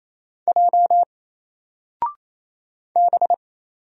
Morse code, heard as a keyed tone, about 28 wpm, 700 Hz. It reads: J B
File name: Single-Letter-Number-28wpm